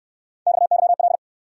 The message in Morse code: XYL

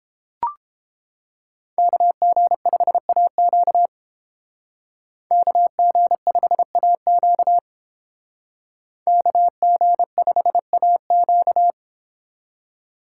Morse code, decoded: KG5AQ KG5AQ KG5AQ

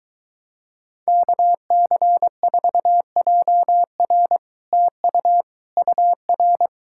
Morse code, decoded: KC4JR TU UR

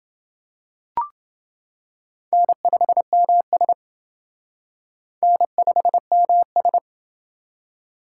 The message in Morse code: N5MS N5MS

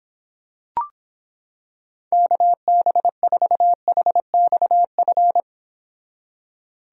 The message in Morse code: KB4HXF

